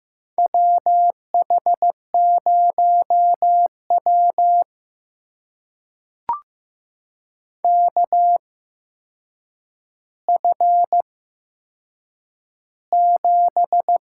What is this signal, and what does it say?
Morse code, heard as a keyed tone, about 15 wpm, spaced as for 7 wpm, 700 Hz. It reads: WH0W KF7
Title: USA Call Signs - 8x Character Spacing - 15wpm